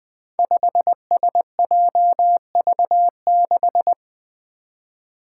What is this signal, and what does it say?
Morse code, heard as a keyed tone, about 21 wpm, 700 Hz. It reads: 5SJV6
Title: Five Challenging Letter-Number Combinations 20wpm